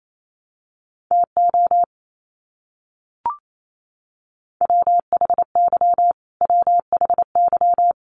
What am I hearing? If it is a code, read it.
Morse code, heard as a keyed tone, about 28 wpm, 700 Hz. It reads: TO WHY WHY
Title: Top 500 Words Repeated 28wpm